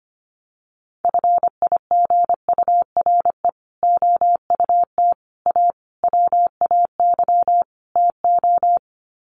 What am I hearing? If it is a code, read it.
Morse code, heard as a keyed tone, about 25 wpm, 700 Hz. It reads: FIGURE OUT A WAY TO
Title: Sets of 5 Words Encore2 25wpm